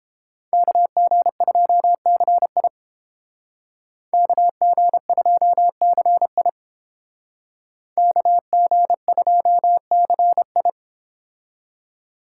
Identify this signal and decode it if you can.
Morse code, heard as a keyed tone, about 30 wpm, 700 Hz. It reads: KG2CI KG2CI KG2CI